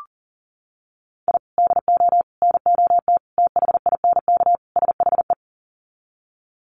Morse code, heard as a keyed tone, about 40 wpm, 700 Hz. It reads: I DO NOT THINK SHE
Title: Sets of 5 Words 40wpm